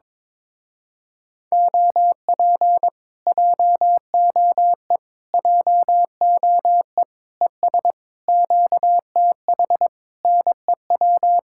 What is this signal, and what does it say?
Morse code, heard as a keyed tone, about 22 wpm, 700 Hz. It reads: OP JOE JOE ES QTH NEW
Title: Speed-Racing Common QSO Phrases Encore - 22wpm